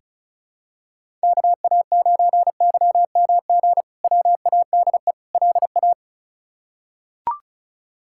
Morse code, heard as a keyed tone, about 36 wpm, 700 Hz. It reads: KA9YMG WADE LA
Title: K1USN SST QSO Practice - 35wpm